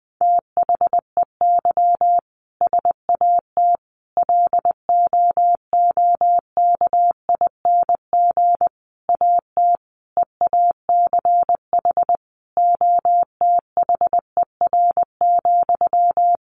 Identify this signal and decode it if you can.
Morse code, heard as a keyed tone, about 20 wpm, 700 Hz. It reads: THEY SAT LOOKING AT EACH OTHER,
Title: A Sound Of Thunder 20wpm